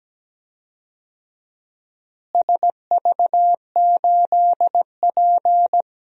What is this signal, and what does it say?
Morse code, heard as a keyed tone, about 17 wpm, 700 Hz. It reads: SV8P
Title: DXCC-100 Call Signs - 4 Characters - Repeated 2x - 17wpm